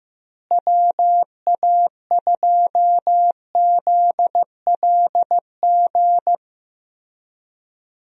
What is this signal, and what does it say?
Morse code, heard as a keyed tone, about 15 wpm, 700 Hz. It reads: WA2ZLG
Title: USA Call Signs - 6 Characters - Repeated 3x - 15wpm